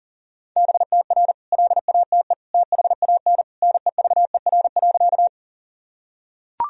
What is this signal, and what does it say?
Morse code, heard as a keyed tone, about 40 wpm, 700 Hz. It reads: BTR LATE THAN NEVER.
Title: Sentences from Top 400 Words and Abbreviations 40wpm